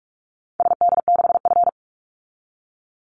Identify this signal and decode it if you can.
Morse code, heard as a keyed tone, about 44 wpm, 700 Hz. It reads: SD6F
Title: Call Signs Repeated 3x - 45wpm